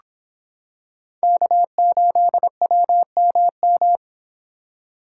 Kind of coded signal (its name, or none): Morse code